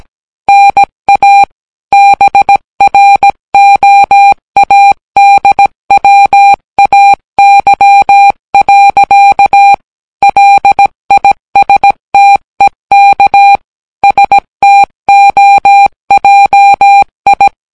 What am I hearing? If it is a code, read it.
Morse code, heard as a keyed tone, about 17 wpm, 800 Hz. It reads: NA BROADWAY. LISTEK STOJI